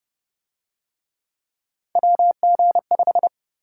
Morse code, heard as a keyed tone, about 26 wpm, 700 Hz. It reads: WG5